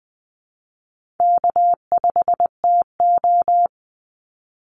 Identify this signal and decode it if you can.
Morse code, heard as a keyed tone, about 20 wpm, 700 Hz. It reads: K5TO
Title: Easy Call Signs Repeated 2x - 20wpm